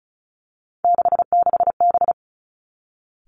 Morse code, heard as a keyed tone, about 36 wpm, 700 Hz. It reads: 66B
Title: Three Challenging Letter-Number Combinations 35wpm